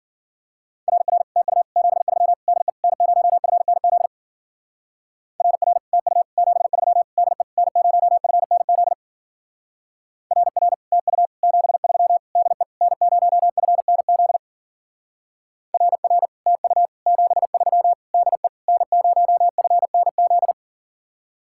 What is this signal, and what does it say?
Morse code, heard as a keyed tone, about 49 wpm, 700 Hz. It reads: RR TU 73 DE N0FNZ RR TU 73 DE N0FNZ RR TU 73 DE N0FNZ RR TU 73 DE N0FNZ
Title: Speed-Racing Common QSO Phrases Encore - 40wpm